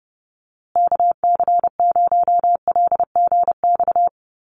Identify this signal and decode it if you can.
Morse code, heard as a keyed tone, about 30 wpm, 700 Hz. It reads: KC0LGX
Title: USA Call Signs Repeated 2x 30wpm